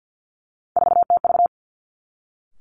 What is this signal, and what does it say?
Morse code, heard as a keyed tone, about 50 wpm, 700 Hz. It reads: 4TV
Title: Three Challenging Letter-Number Combinations 50wpm